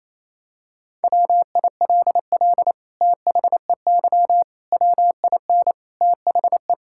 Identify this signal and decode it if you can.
Morse code, heard as a keyed tone, about 28 wpm, 700 Hz. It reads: WILL THEY WIN THE
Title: Sentences from Top 1600 Words 28wpm